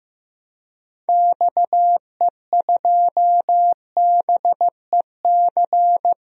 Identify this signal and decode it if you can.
Morse code, heard as a keyed tone, about 15 wpm, 700 Hz. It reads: XE2BEC
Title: DXCC-100 Call Signs - 6 Characters - 15wpm